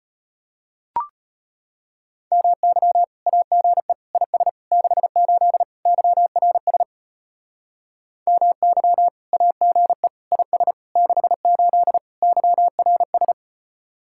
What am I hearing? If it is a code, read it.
Morse code, heard as a keyed tone, about 33 wpm, 700 Hz. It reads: MY AGE IS 68 YRS MY AGE IS 68 YRS